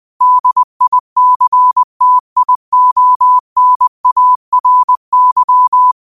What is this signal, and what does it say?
Morse code, heard as a keyed tone, about 20 wpm, 1000 Hz. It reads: DICTIONARY
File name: qotdfiles / 20wpm